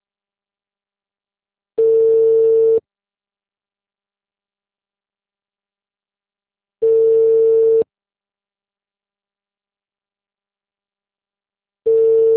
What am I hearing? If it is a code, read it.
Ringback tone (China, 450 Hz).
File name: out- -2015- - - .202